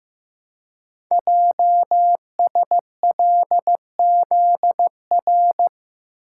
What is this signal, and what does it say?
Morse code, heard as a keyed tone, about 15 wpm, 700 Hz. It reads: JSLZR
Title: Code Groups - 5 Letters - 15wpm